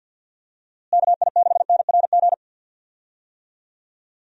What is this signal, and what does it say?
Morse code, heard as a keyed tone, about 52 wpm, 700 Hz. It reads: KI6NRG
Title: USA Call Signs - 6 Characters - Repeated 3x - 50wpm